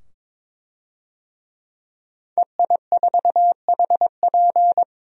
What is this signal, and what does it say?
Morse code, heard as a keyed tone, about 23 wpm, 700 Hz. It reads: EI4HP
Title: DXCC-100 Call Signs - 5 Characters - 22wpm